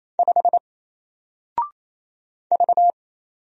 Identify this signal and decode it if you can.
Morse code, heard as a keyed tone, about 27 wpm, 700 Hz. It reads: 5 V
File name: Single-Letter-Number-28wpm